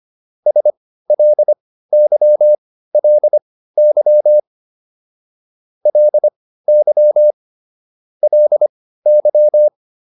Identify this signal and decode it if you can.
Morse code, heard as a keyed tone, about 25 wpm, spaced as for 15 wpm, 600 Hz. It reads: SLYLY LY LY